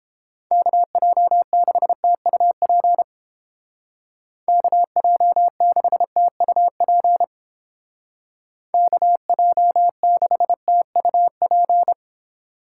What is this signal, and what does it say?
Morse code, heard as a keyed tone, about 29 wpm, 700 Hz. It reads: KJ6TUP KJ6TUP KJ6TUP